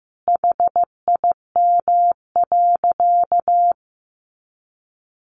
Morse code, heard as a keyed tone, about 15 wpm, 700 Hz. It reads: HIM.
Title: Easy Phrases 15wpm